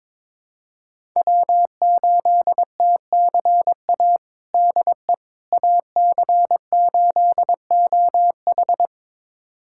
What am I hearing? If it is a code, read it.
Morse code, heard as a keyed tone, about 22 wpm, 700 Hz. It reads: W8TCA DE AC8OH
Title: USA Callsign DE Callsign - 5 Characters - 22wpm